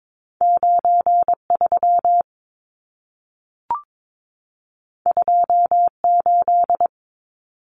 Morse code, heard as a keyed tone, about 22 wpm, 700 Hz. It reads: 93 28
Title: Numbers - 2-Digit 22wpm